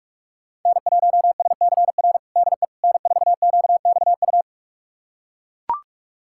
Morse code, heard as a keyed tone, about 42 wpm, 700 Hz. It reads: N1SCR DE N4QXU
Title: USA Callsign DE Callsign - 5 Characters - 45wpm